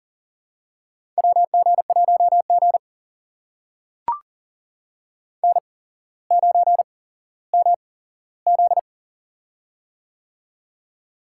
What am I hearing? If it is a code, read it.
Morse code, heard as a keyed tone, about 43 wpm, spaced as for 17 wpm, 700 Hz. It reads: WG1G N9MZ
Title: USA Call Signs - 4 Characters - 8x Character Spacing - 40wpm